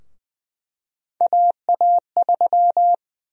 Morse code, heard as a keyed tone, about 20 wpm, 700 Hz. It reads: AA3
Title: USA Call Sign Prefixes - Repeated 2x - 20wpm